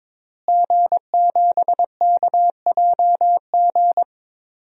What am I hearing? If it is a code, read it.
Morse code, heard as a keyed tone, about 21 wpm, 700 Hz. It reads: G7KJG
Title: DXCC-100 Call Signs - 5 Characters - Repeated 3x - 22wpm